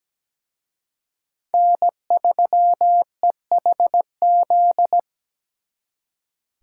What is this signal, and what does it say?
Morse code, heard as a keyed tone, about 20 wpm, 700 Hz. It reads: N3EHZ
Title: Speed Racing - USA 1x3 Call Signs 17wpm